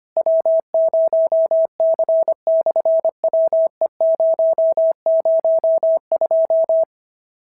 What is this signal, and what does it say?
Morse code, heard as a keyed tone, about 25 wpm, 650 Hz. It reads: W0C/WE002
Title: SOTA QSO Practice 25wpm